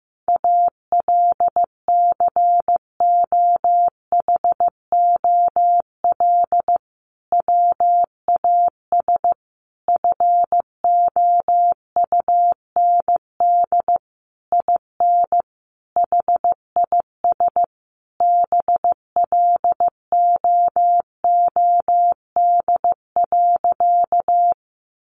Morse code, heard as a keyed tone, about 14 wpm, 700 Hz. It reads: ALCOHOL WAS FOUND IN HIS BLOOD.
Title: Sentences from Top 1400 Words 15wpm